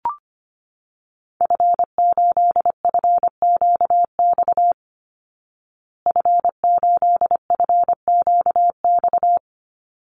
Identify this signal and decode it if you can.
Morse code, heard as a keyed tone, about 25 wpm, 700 Hz. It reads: F8FQX F8FQX
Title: Call Signs Repeated 3x - 25wpm